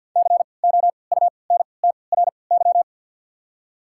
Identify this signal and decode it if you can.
Morse code, heard as a keyed tone, about 53 wpm, spaced as for 29 wpm, 690 Hz. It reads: COUNTRY
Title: 3x Character Spacing - Top 300 Words - 50wpm